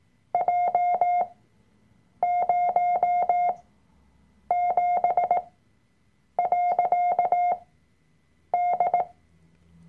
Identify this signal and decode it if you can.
Morse code, heard as a keyed tone, about 18 wpm, spaced as for 7 wpm, 710 Hz. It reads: J07.D
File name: Workshop Dateien / CW-Online-Kurs / Lektion 003